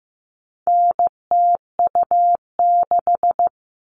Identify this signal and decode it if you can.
Morse code, heard as a keyed tone, about 15 wpm, 700 Hz. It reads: NTU6
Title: All Cut Numbers - 4-Digit 15wpm